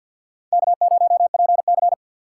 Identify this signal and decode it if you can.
Morse code, heard as a keyed tone, about 52 wpm, 700 Hz. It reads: K0PC
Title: K1USN SST QSO Practice - 50wpm